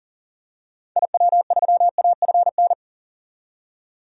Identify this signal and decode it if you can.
Morse code, heard as a keyed tone, about 40 wpm, 700 Hz. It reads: IW3AFN
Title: DXCC-100 Call Signs - 6 Characters - 40wpm